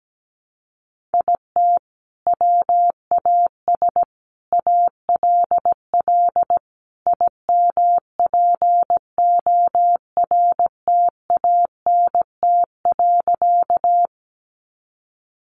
Morse code, heard as a keyed tone, about 17 wpm, 700 Hz. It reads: IT WAS ALL IMPORTANT.
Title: Sentences from Top 600 Words 17wpm